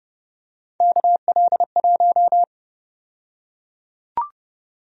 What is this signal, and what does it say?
Morse code, heard as a keyed tone, about 32 wpm, 700 Hz. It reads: KL1
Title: USA Call Sign Prefixes - 30wpm